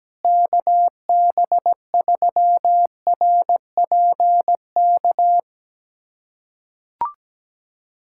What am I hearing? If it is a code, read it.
Morse code, heard as a keyed tone, about 17 wpm, 700 Hz. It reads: KB3RPK
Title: USA Callsign DE Callsign 17wpm